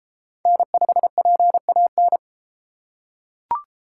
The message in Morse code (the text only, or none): N5PAN